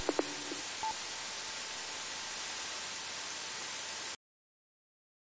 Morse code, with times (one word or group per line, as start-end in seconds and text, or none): none